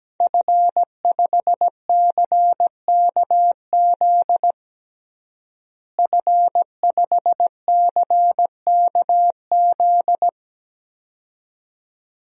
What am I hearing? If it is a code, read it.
Morse code, heard as a keyed tone, about 17 wpm, 700 Hz. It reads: F5CKZ F5CKZ
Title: DXCC-100 Call Signs - 5 Characters - Repeated 2x - 17wpm